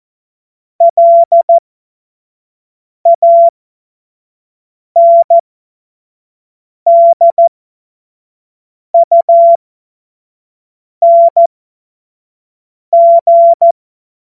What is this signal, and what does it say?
Morse code, heard as a keyed tone, about 14 wpm, spaced as for 5 wpm, 680 Hz. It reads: LANDUNG
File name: Morsekurs / Witze / Witz 20. Juni 2023 25 F70